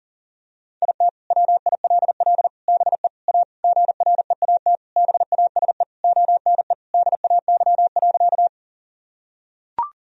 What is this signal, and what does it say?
Morse code, heard as a keyed tone, about 37 wpm, 700 Hz. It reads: IT WILL BE A GREAT BASE ONE DAY.
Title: Sentences from Top 1000 Words - Review Part 1 - 40wpm